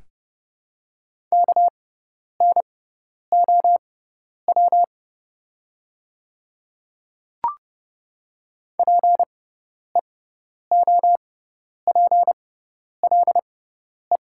Morse code, heard as a keyed tone, about 30 wpm, spaced as for 10 wpm, 700 Hz. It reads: KNOW PEOPLE
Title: 6x Character Spacing - Top 100 Words - 30wpm